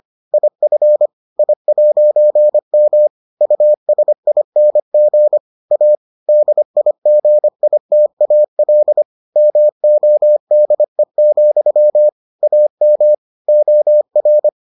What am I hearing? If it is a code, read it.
Morse code, heard as a keyed tone, about 25 wpm, 600 Hz. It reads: IF I'M USING A DIGITAL MODE, AM OR